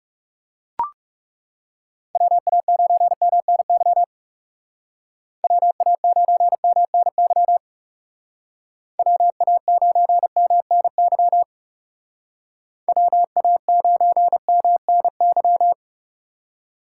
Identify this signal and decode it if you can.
Morse code, heard as a keyed tone, about 36 wpm, 700 Hz. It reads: WA9MNY WA9MNY WA9MNY WA9MNY
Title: Speed Racing - USA 2x3 Call Signs 30wpm